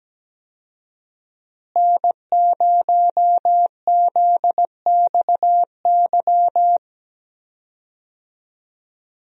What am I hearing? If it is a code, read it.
Morse code, heard as a keyed tone, about 17 wpm, 700 Hz. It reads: N0ZXY
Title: USA Call Signs - 5 Characters - Repeated 3x - 17wpm